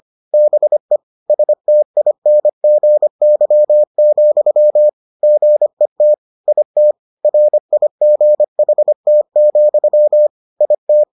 Morse code, heard as a keyed tone, about 25 wpm, 600 Hz. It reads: BE STINGY, GET IT RIGHT, IT